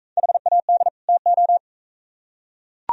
Morse code, heard as a keyed tone, about 42 wpm, 700 Hz. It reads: HAD TO